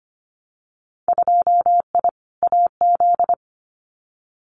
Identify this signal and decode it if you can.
Morse code, heard as a keyed tone, about 25 wpm, 700 Hz. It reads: 2I AZ
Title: Winter Field Day Exchanges 25wpm